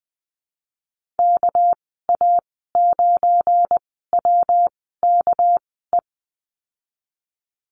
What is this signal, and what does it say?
Morse code, heard as a keyed tone, about 20 wpm, spaced as for 14 wpm, 700 Hz. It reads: KA9WKE
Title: 200 Lesson 32g - Letter k - Call Signs 20wpm